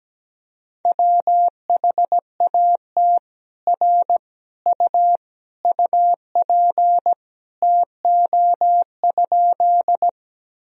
Morse code, heard as a keyed tone, about 17 wpm, 700 Hz. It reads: WHAT R U UP TO?
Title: Sentences from Top 100 Words and Abbreviations 17wpm